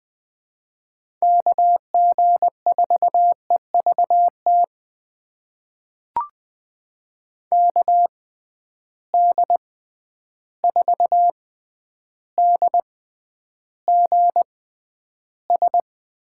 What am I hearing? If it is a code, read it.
Morse code, heard as a keyed tone, about 20 wpm, spaced as for 10 wpm, 700 Hz. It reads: KG4EVT KD4DGS
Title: USA Call Signs - 6x Character Spacing - 20wpm